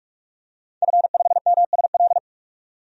Morse code, heard as a keyed tone, about 46 wpm, 700 Hz. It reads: F5MSL
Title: DXCC-100 Call Signs - 5 Characters - 45wpm